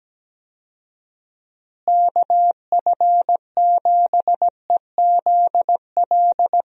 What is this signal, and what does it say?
Morse code, heard as a keyed tone, about 17 wpm, 700 Hz. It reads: KF7EZL
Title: USA Call Signs - 6 Characters - Repeated 2x - 17wpm